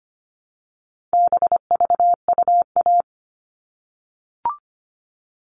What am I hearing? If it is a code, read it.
Morse code, heard as a keyed tone, about 25 wpm, 700 Hz. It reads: BVUA
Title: All Cut Numbers - 4-Digit 25wpm